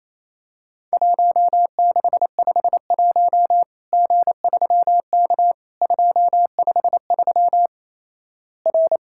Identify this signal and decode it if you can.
Morse code, heard as a keyed tone, about 27 wpm, 700 Hz. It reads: 1651 G3K 253 R
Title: Worked All Europe DX Contest - QTC Exchange - 28wpm